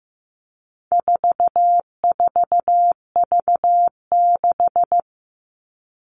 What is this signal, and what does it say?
Morse code, heard as a keyed tone, about 15 wpm, 700 Hz. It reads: 44V6
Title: All Cut Numbers - 4-Digit 15wpm